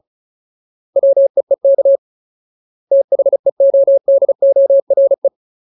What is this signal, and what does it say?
Morse code, heard as a keyed tone, about 34 wpm, 550 Hz. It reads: WEEK THEODORE
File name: Ultimate Word List / 35WPM Ultimate